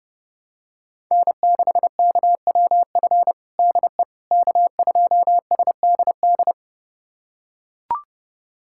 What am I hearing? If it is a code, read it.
Morse code, heard as a keyed tone, about 32 wpm, 700 Hz. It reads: N6KWF DE K2SDD
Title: USA Callsign DE Callsign - 5 Characters - 30wpm